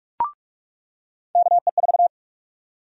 Morse code, heard as a keyed tone, about 43 wpm, 700 Hz. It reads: KE4
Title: USA Call Sign Prefixes - 45wpm